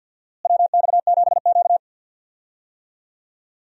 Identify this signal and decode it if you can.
Morse code, heard as a keyed tone, about 52 wpm, 700 Hz. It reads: WX7Q
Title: USA Call Signs - 4 Characters - Repeated 3x - 50wpm